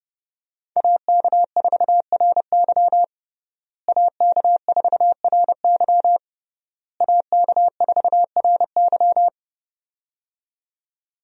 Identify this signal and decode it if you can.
Morse code, heard as a keyed tone, about 30 wpm, 700 Hz. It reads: AK4RY AK4RY AK4RY
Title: USA Call Signs - 5 Characters - Repeated 3x - 30wpm